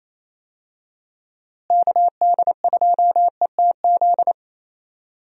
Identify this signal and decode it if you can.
Morse code, heard as a keyed tone, about 28 wpm, 700 Hz. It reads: KD2ETZ
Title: USA Call Signs - 6 Characters - Repeated 3x - 28wpm